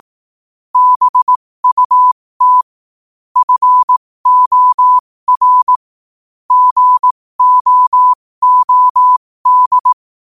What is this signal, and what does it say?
Morse code, heard as a keyed tone, about 18 wpm, spaced as for 15 wpm, 1000 Hz. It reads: BUT FOR GOOD
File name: qotdfiles / 15wpm